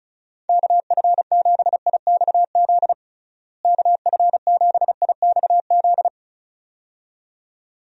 Morse code, heard as a keyed tone, about 36 wpm, 700 Hz. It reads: KF7IXZ KF7IXZ
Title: USA Call Signs - 6 Characters - Repeated 2x - 35wpm